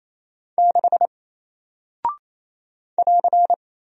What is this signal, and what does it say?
Morse code, heard as a keyed tone, about 28 wpm, 700 Hz. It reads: 6 <AR>
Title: Single Character (Letter:Number:Punctuation:Prosign) Alt 28wpm